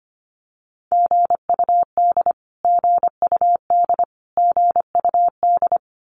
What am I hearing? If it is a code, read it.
Morse code, heard as a keyed tone, about 25 wpm, 700 Hz. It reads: GUD GUD GUD
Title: Abbreviations Repeated 3x - 25wpm